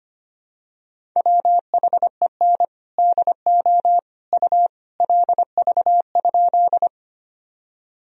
Morse code, heard as a keyed tone, about 25 wpm, 700 Hz. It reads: WHEN DO U LV?
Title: Sentences from Top 400 Words and Abbreviations 25wpm